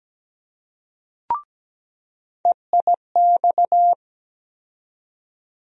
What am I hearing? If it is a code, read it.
Morse code, heard as a keyed tone, about 17 wpm, 700 Hz. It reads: EIX
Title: USA Call Sign Postfixes - 3 Letters 17wpm